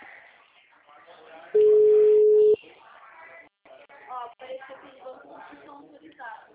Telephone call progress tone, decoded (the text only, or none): ringback tone